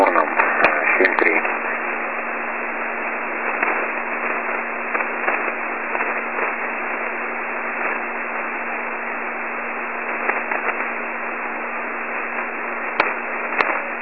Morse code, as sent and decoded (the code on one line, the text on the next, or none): none